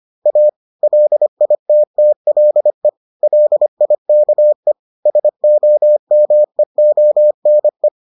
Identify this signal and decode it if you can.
Morse code, heard as a keyed tone, about 25 wpm, 600 Hz. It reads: A LITTLE LIKE SOMEONE